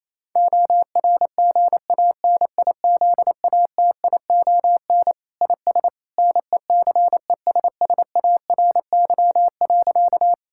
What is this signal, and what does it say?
Morse code, heard as a keyed tone, about 28 wpm, 700 Hz. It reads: ORGANIZATION IS NECESSARY.
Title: Sentences from Top 1000 Words - Review Part 5 - 28wpm